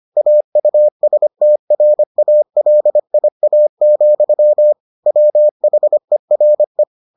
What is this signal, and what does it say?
Morse code, heard as a keyed tone, about 25 wpm, 600 Hz. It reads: AUSTRALIA, WHERE